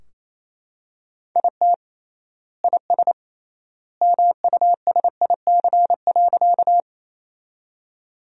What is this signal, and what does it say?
Morse code, heard as a keyed tone, about 28 wpm, 700 Hz. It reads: IT IS MUSIC.
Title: 3x Word-Spacing - Sentences from Top 300 Words - 28wpm